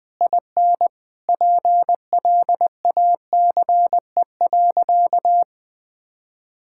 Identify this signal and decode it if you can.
Morse code, heard as a keyed tone, about 21 wpm, 700 Hz. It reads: IN PLACE.
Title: Sentences from Top 1000 Words - Review Part 6 - 20wpm